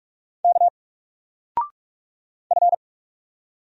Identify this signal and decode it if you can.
Morse code, heard as a keyed tone, about 44 wpm, 700 Hz. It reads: K F
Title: Single Letter-Number 45wpm